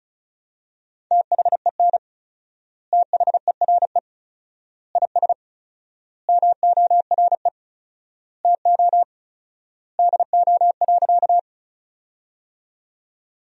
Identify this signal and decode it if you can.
Morse code, heard as a keyed tone, about 36 wpm, 700 Hz. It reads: THEN THERE IS MORE TO DO.
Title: 4x Word-Spacing - Sentences from Top 100 Words - 35wpm